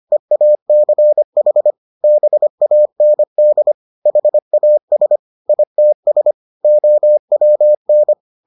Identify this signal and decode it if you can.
Morse code, heard as a keyed tone, about 25 wpm, 600 Hz. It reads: EACH BAND HAS ITS OWN